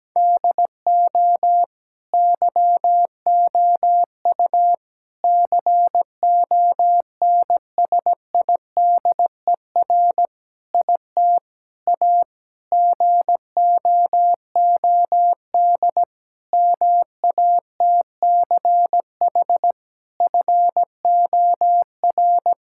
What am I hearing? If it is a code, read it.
Morse code, heard as a keyed tone, about 17 wpm, 700 Hz. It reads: DO YOU CONSIDER IT A GOOD MATCH FOR
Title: Sentences from Top 1600 Words 17wpm